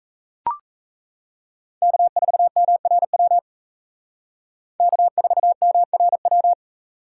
Morse code, heard as a keyed tone, about 33 wpm, 700 Hz. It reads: K4MRW K4MRW
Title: Speed Racing - USA Call Signs 28wpm